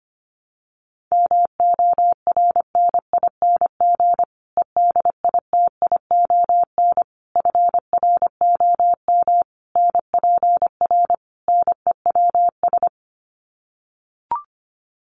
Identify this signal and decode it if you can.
Morse code, heard as a keyed tone, about 25 wpm, 700 Hz. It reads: MORNING EDITION FROM NPR NEWS
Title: Sets of 5 Words Encore 25wpm